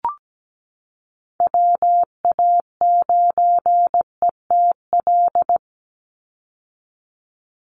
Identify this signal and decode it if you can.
Morse code, heard as a keyed tone, about 17 wpm, 700 Hz. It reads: WA9ETL
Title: USA Call Signs 17wpm